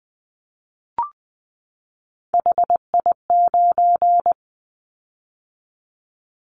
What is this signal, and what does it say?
Morse code, heard as a keyed tone, about 20 wpm, 700 Hz. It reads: HI9
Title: Callsign Prefixes for Top 100 DXCC - 20wpm